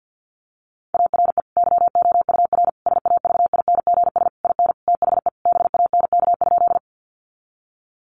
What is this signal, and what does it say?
Morse code, heard as a keyed tone, about 50 wpm, 700 Hz. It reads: ARE YOUR SAVINGS IN THE BANK?
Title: Sentences from Top 1200 Words 50wpm